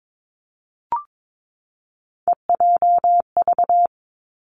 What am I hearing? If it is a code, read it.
Morse code, heard as a keyed tone, about 22 wpm, 700 Hz. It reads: EJV
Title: Three Challenging Letter-Number Combinations 22wpm